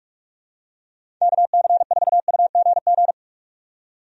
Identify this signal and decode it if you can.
Morse code, heard as a keyed tone, about 43 wpm, 700 Hz. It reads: KC4UGG
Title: USA Call Signs - 6 Characters - 45wpm